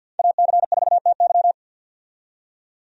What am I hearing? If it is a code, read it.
Morse code, heard as a keyed tone, about 52 wpm, 700 Hz. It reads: AC4TY
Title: USA Call Signs - 5 Characters - 50wpm